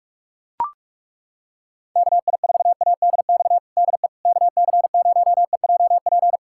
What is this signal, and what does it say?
Morse code, heard as a keyed tone, about 41 wpm, 700 Hz. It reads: KI4ADX DE KC0EJP
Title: USA Callsign DE Callsign - 6 Characters - 45wpm